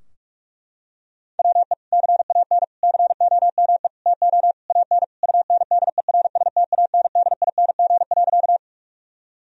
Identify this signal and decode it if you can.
Morse code, heard as a keyed tone, about 48 wpm, 700 Hz. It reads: WE CAN COME TO AN UNDERSTANDING.
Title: Sentences from Top 1000 Words - Review Part 3 - 45wpm